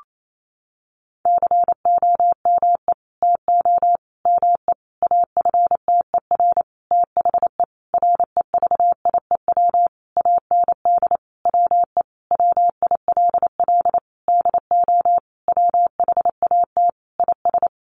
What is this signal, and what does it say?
Morse code, heard as a keyed tone, about 28 wpm, 690 Hz. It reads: COME TO ME AFTER THE REVIEW AND WE WILL DO WHAT IS
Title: Sentences from Top 900 Words 28wpm